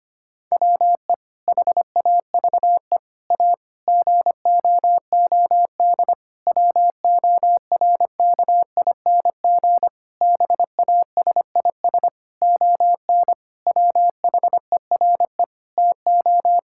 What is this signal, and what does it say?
Morse code, heard as a keyed tone, about 25 wpm, 700 Hz. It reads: WE HAVE A GOOD WORKING BASIS ON WHERE TO
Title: Sentences from Top 1000 Words - Review Part 5 - 25wpm